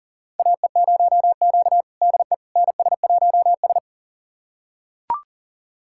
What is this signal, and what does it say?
Morse code, heard as a keyed tone, about 40 wpm, 700 Hz. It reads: AE0Q DE NS1S
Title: USA Callsign DE Callsign - 4 Characters - 40wpm